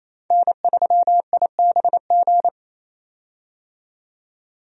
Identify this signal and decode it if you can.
Morse code, heard as a keyed tone, about 28 wpm, 700 Hz. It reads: N3IBG
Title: USA Call Signs - 5 Characters - 28wpm